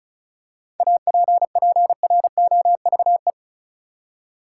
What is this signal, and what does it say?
Morse code, heard as a keyed tone, about 34 wpm, 700 Hz. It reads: APPROVE
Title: Seven Letter Words - 35wpm